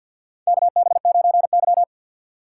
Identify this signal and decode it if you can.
Morse code, heard as a keyed tone, about 50 wpm, 700 Hz. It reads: KB9Y
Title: USA Call Signs - 4 Characters - 6x Character Spacing - 50wpm